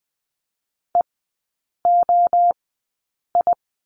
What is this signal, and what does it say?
Morse code, heard as a keyed tone, about 20 wpm, spaced as for 8 wpm, 700 Hz. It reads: EOI
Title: 053 Lesson 10e - Letter r - Three Characters 20wpm